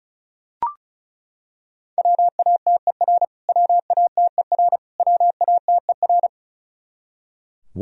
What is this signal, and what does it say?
Morse code, heard as a keyed tone, about 36 wpm, 700 Hz. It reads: WATER WATER WATER
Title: Top 100 Words Repeated 35wpm